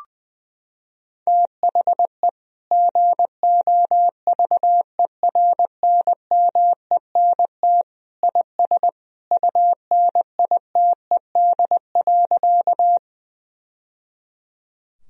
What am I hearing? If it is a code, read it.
Morse code, heard as a keyed tone, about 20 wpm, 700 Hz. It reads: THE GOVERNMENT IS UNITED.
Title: Sentences from Top 1000 Words - Review Part 4 - 20wpm